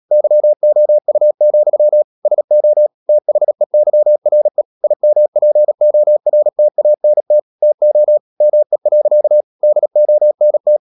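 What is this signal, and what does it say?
Morse code, heard as a keyed tone, about 37 wpm, 600 Hz. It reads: YOU, SO THEYRE IMPORTANT TO ME. DONT
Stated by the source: Morse practice at 36 wpm, text of Ken Follett's novel Never, from CW med Gnister.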